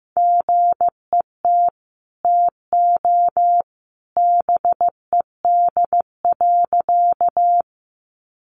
Morse code, heard as a keyed tone, about 15 wpm, 700 Hz. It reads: GET TO BED.
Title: Sentences from Top 500 Words 15wpm